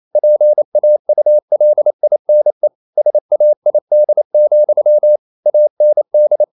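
Morse code, heard as a keyed tone, about 28 wpm, 600 Hz. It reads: PAULINE SAID, AND
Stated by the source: Morse code practice at 28 wpm, text of Ken Follett's novel Never in English, from CW med Gnister.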